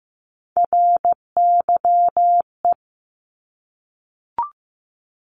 Morse code, heard as a keyed tone, about 15 wpm, 700 Hz. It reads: RYE
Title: Multi-letter Phonemes 15wpm